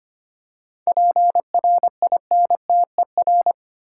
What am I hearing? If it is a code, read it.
Morse code, heard as a keyed tone, about 25 wpm, 700 Hz. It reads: PRINTER
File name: practice / 25wpm / Easy Words with Hints - 25wpm